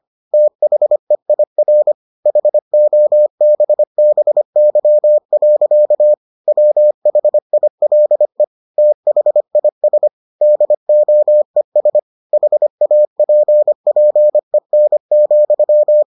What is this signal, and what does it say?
Morse code, heard as a keyed tone, about 25 wpm, 600 Hz. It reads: THEIR HOBBY. WHILE THIS DOES HAPPEN,